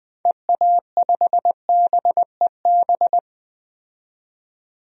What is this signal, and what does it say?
Morse code, heard as a keyed tone, about 20 wpm, 700 Hz. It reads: EA5BEB
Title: DXCC-100 Call Signs - 6 Characters - 20wpm